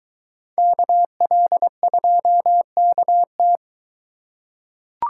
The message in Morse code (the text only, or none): KL2KT